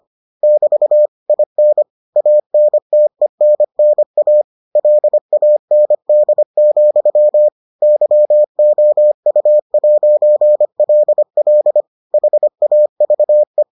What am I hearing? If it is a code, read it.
Morse code, heard as a keyed tone, about 25 wpm, 600 Hz. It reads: <BT> IN ANTENNA LAND, YOU'LL HAVE